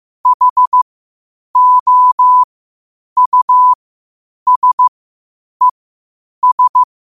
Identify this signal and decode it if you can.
Morse code, heard as a keyed tone, about 15 wpm, spaced as for 8 wpm, 1000 Hz. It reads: HOUSES